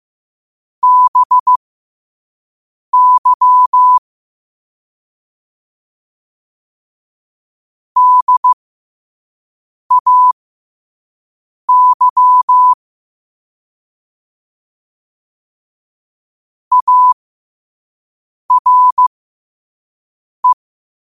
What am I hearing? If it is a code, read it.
Morse code, heard as a keyed tone, about 15 wpm, spaced as for 5 wpm, 1000 Hz. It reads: BY DAY ARE
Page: New quotes every day in morse code at 5 Words per minute.